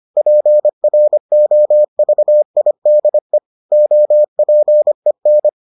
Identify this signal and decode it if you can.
Morse code, heard as a keyed tone, about 25 wpm, 600 Hz. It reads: PROVIDE OPEN